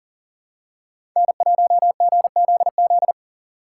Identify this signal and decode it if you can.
Morse code, heard as a keyed tone, about 40 wpm, 700 Hz. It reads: N1GZZ
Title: USA Call Signs - 5 Characters - 3x Character Spacing - 40wpm